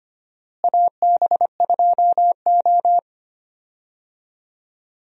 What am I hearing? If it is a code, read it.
Morse code, heard as a keyed tone, about 25 wpm, 700 Hz. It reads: AB2O
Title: USA Call Signs - 4 Characters - Repeated 3x - 25wpm